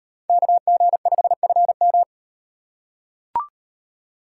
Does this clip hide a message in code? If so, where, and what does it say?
Yes: Morse code from 0.3 to 2.0 s, heard as a keyed tone, about 38 wpm, 700 Hz. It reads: KG5FM